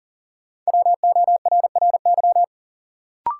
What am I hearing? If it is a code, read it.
Morse code, heard as a keyed tone, about 40 wpm, 700 Hz. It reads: WORRY
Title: Sets of 5 Words - New Word Drill - 40wpm